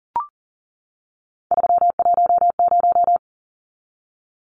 Morse code, heard as a keyed tone, about 40 wpm, 700 Hz. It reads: 310
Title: Numbers - 3-Digit 40wpm